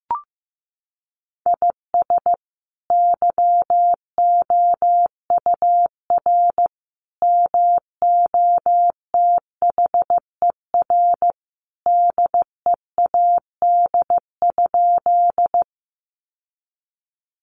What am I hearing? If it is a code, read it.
Morse code, heard as a keyed tone, about 15 wpm, 700 Hz. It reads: IS YOUR MOTHER DEAD?
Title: Sentences from Top 1500 Words 15wpm